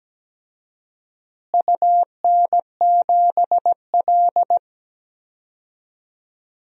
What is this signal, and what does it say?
Morse code, heard as a keyed tone, about 17 wpm, 700 Hz. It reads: UN7L
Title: DXCC-100 Call Signs - 4 Characters - Repeated 2x - 17wpm